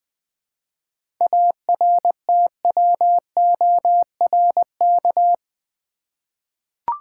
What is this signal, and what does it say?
Morse code, heard as a keyed tone, about 20 wpm, 700 Hz. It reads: ARTWORK
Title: 7 Letter Words - First 3 Letters Given - 20wpm